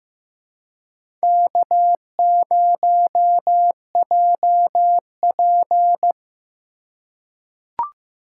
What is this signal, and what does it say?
Morse code, heard as a keyed tone, about 15 wpm, 700 Hz. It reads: K0JP
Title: USA Call Signs - 4 Characters - 2x Character Spacing - 15wpm